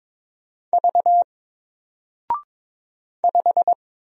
Morse code, heard as a keyed tone, about 22 wpm, 700 Hz. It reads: V 5
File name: Single-Letter-Number-22wpm